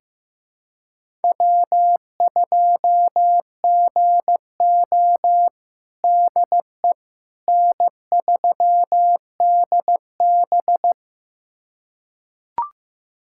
Morse code, heard as a keyed tone, about 16 wpm, 700 Hz. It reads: W2GO DE N3DB
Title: USA Callsign DE Callsign - 4 Characters - 15wpm